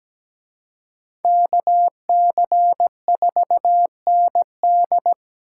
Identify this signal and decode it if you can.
Morse code, heard as a keyed tone, about 20 wpm, 700 Hz. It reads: KC4ND
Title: Speed Racing - USA 2x2 Call Signs 17wpm